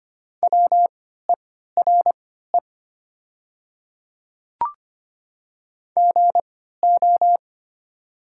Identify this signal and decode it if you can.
Morse code, heard as a keyed tone, about 25 wpm, spaced as for 14 wpm, 700 Hz. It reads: WERE GO
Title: 3x Character Spacing - Top 100 Words - 25wpm